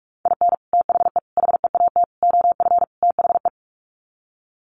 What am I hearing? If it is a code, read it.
Morse code, heard as a keyed tone, about 44 wpm, 690 Hz. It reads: IN THE HEAT OF THE
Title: Sets of 5 Words Encore2 45wpm